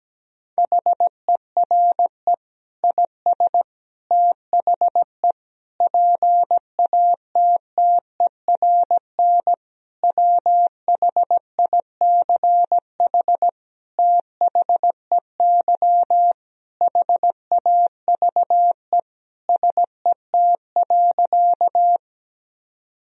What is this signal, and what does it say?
Morse code, heard as a keyed tone, about 17 wpm, 700 Hz. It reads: HERE IS THE PATTERN WHICH THEY HAVE SET.
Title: Sentences from Top 1000 Words - Review Part 2 - 17wpm